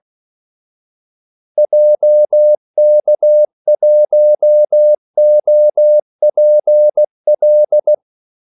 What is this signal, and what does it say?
Morse code, heard as a keyed tone, about 16 wpm, 600 Hz. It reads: JK1OPL
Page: Kaldesignaler 16 wpm